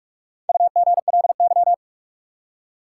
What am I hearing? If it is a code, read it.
Morse code, heard as a keyed tone, about 41 wpm, 700 Hz. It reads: UGLY
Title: Four Letter Words - 45wpm